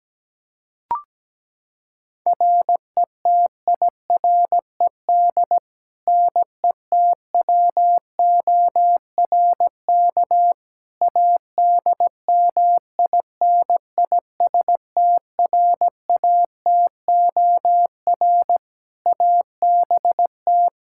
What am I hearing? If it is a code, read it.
Morse code, heard as a keyed tone, about 17 wpm, 700 Hz. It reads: RETIRED NETWORK ADMINISTRATOR ABT
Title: Practice QSO - Exchange 3 - More Info - 17wpm